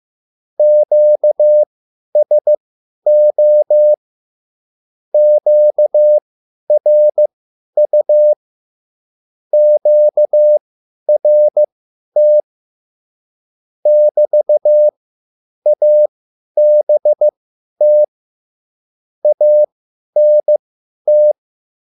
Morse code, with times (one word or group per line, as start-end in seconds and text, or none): 0.6-3.9 QSO
5.1-8.3 QRU
9.5-12.4 QRT
13.9-18.0 <BT>ABT
19.2-21.3 ANT